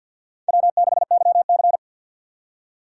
Morse code, heard as a keyed tone, about 65 wpm, 700 Hz. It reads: W6YC